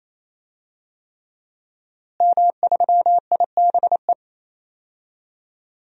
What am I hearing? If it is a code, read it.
Morse code, heard as a keyed tone, about 28 wpm, 700 Hz. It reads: M3IBE
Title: DXCC-100 Call Signs - 5 Characters - Repeated 3x - 28wpm